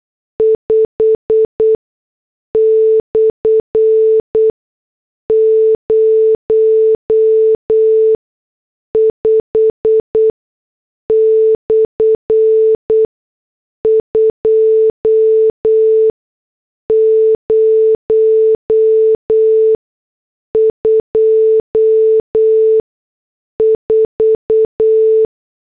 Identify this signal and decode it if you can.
Morse code, heard as a keyed tone, about 8 wpm, spaced as for 6 wpm, 440 Hz. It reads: 5/05/2024